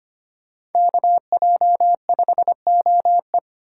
Morse code, heard as a keyed tone, about 33 wpm, 700 Hz. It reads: KJ5OE